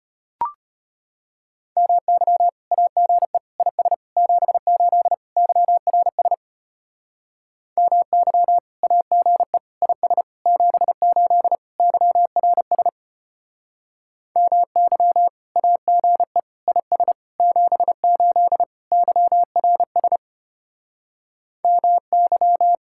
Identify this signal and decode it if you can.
Morse code, heard as a keyed tone, about 32 wpm, 700 Hz. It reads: MY AGE IS 78 YRS MY AGE IS 78 YRS MY AGE IS 78 YRS MY